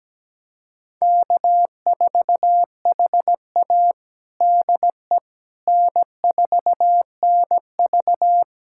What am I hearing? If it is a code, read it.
Morse code, heard as a keyed tone, about 17 wpm, 700 Hz. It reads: K4HA DE N4NV
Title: USA Callsign DE Callsign - 4 Characters - 17wpm